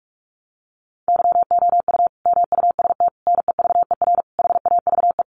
Morse code, heard as a keyed tone, about 44 wpm, 700 Hz. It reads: YOU MUST NEVER HAVE
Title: Sentences from Top 1000 Words 45wpm